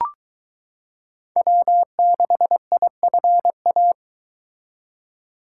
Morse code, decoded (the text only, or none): W6IFA